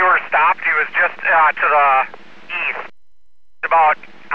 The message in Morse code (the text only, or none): none